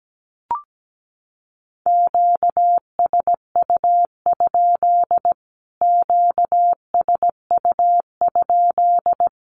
Morse code, heard as a keyed tone, about 17 wpm, 700 Hz. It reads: QSU? QSU?
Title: Q-Codes Repeated 3x 17wpm